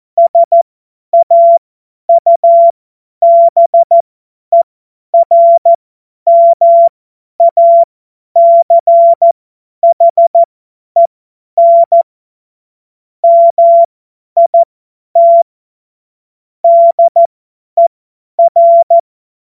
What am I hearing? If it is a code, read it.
Morse code, heard as a keyed tone, about 14 wpm, spaced as for 10 wpm, 680 Hz. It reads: SAUBERMACHEN MIT DER
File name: Morsekurs / Witze / Witz 13 Mai 2025 50 F70